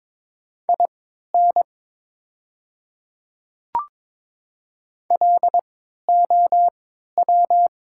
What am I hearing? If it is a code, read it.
Morse code, heard as a keyed tone, about 22 wpm, spaced as for 12 wpm, 700 Hz. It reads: IN LOW
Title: 3x Character Spacing - Top 200 Words - 22wpm